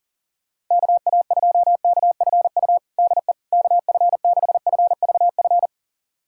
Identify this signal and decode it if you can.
Morse code, heard as a keyed tone, about 40 wpm, 700 Hz. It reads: KA2KFU DE KF6FVF
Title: USA Callsign DE Callsign - 6 Characters - 40wpm